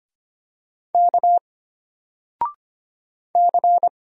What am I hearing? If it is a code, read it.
Morse code, heard as a keyed tone, about 25 wpm, 700 Hz. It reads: K C
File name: Single-Letter-Number-25wpm